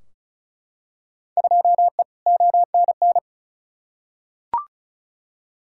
Morse code, heard as a keyed tone, about 36 wpm, 700 Hz. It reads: 2E ONN
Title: Field Day Exchanges Encore - 35wpm